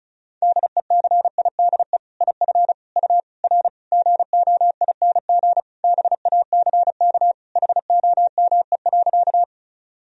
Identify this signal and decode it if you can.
Morse code, heard as a keyed tone, about 36 wpm, 700 Hz. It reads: DECIDE IF U R GOING BACK HOME.
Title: Sentences from Top 500 Words and Abbreviations 35wpm